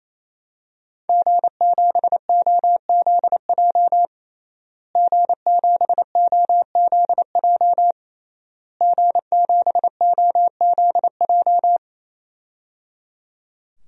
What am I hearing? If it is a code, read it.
Morse code, heard as a keyed tone, about 28 wpm, 700 Hz. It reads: G7OZJ G7OZJ G7OZJ
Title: DXCC-100 Call Signs - 5 Characters - Repeated 3x - 28wpm